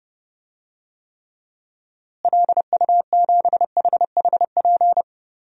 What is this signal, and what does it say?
Morse code, heard as a keyed tone, about 30 wpm, 700 Hz. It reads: LU7HHP
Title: DXCC-100 Call Signs - 6 Characters - Repeated 2x - 30wpm